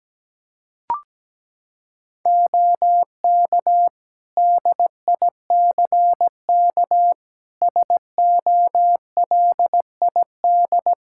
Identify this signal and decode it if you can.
Morse code, heard as a keyed tone, about 17 wpm, 700 Hz. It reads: OK DICK SOLID
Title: Practice QSO - Exchange 3 - More Info - 17wpm